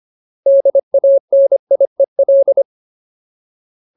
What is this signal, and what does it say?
Morse code, heard as a keyed tone, about 25 wpm, 550 Hz. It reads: DANIEL
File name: Ultimate Word List / 25WPM Ultimate